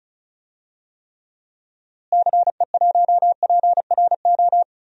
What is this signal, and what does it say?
Morse code, heard as a keyed tone, about 36 wpm, 700 Hz. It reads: CE1PRO
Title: DXCC-100 Call Signs - 6 Characters - Repeated 2x - 35wpm